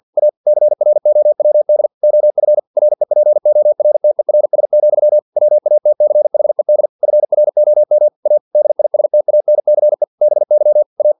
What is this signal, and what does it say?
Morse code, heard as a keyed tone, about 50 wpm, 600 Hz. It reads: A CROWD OF REPORTERS, WATCHED FROM A DISTANCE BY A
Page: Never 49wpm